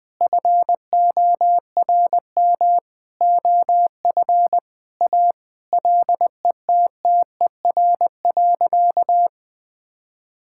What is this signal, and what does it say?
Morse code, heard as a keyed tone, about 20 wpm, 700 Hz. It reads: FORM OF A LETTER.
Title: Sentences with 5-Word Phrases - 20wpm